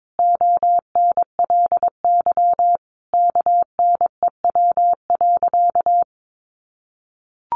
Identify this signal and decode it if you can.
Morse code, heard as a keyed tone, about 22 wpm, 700 Hz. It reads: ONLY KNEW.
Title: Sentences from Top 500 Words 22wpm